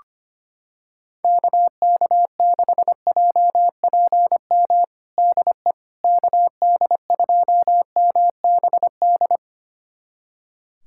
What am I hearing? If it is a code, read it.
Morse code, heard as a keyed tone, about 25 wpm, 700 Hz. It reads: KK6JPM DE KD2MBD
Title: USA Callsign DE Callsign - 6 Characters - 25wpm